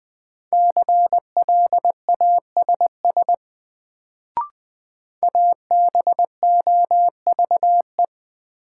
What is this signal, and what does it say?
Morse code, heard as a keyed tone, about 21 wpm, 700 Hz. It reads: CLASS ABOVE
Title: Sentences from Top 1000 Words - Review Part 2 - New Word Drill - 20wpm